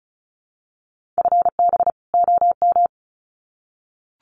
Morse code, heard as a keyed tone, about 34 wpm, 700 Hz. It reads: FB OM
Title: Common QSO Phrases 35wpm